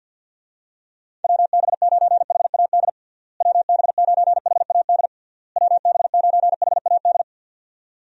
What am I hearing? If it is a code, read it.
Morse code, heard as a keyed tone, about 50 wpm, 700 Hz. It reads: WB9HAD WB9HAD WB9HAD
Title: USA Call Signs - 6 Characters - Repeated 3x - 50wpm